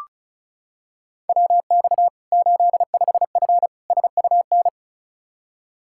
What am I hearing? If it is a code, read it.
Morse code, heard as a keyed tone, about 34 wpm, 700 Hz. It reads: WX 85F SUN
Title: Practice QSO - Exchange 2 - Info - 35wpm